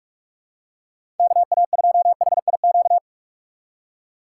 Morse code, heard as a keyed tone, about 47 wpm, 700 Hz. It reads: KA2HIQ
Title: USA Call Signs - 4x Character Spacing - 45wpm